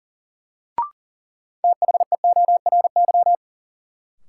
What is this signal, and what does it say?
Morse code, heard as a keyed tone, about 40 wpm, 700 Hz. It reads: THEORY
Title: Sentences from Top 600 Words - New Word Drill - 40wpm